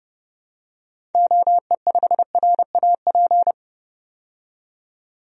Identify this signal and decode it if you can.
Morse code, heard as a keyed tone, about 30 wpm, 700 Hz. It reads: OE5RAP
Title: DXCC-100 Call Signs - 6 Characters - 30wpm